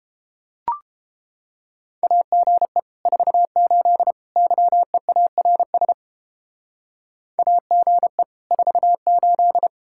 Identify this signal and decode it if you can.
Morse code, heard as a keyed tone, about 29 wpm, 700 Hz. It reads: AGE 48 YEARS AGE 48